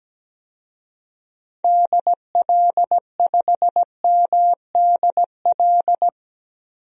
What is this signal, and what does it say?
Morse code, heard as a keyed tone, about 17 wpm, 700 Hz. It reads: DL5MDL
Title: DXCC-100 Call Signs - 6 Characters - Repeated 2x - 17wpm